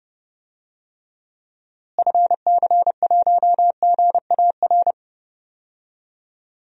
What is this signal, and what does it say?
Morse code, heard as a keyed tone, about 30 wpm, 700 Hz. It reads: FC1GAR
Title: DXCC-100 Call Signs - 6 Characters - Repeated 3x - 30wpm